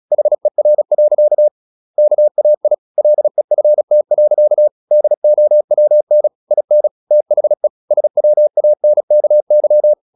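Morse code, heard as a keyed tone, about 36 wpm, 600 Hz. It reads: HER. KAI LEFT. DOWN IN THE SWANKY
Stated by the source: Morse practice at 36 wpm, text of Ken Follett's novel Never, from CW med Gnister.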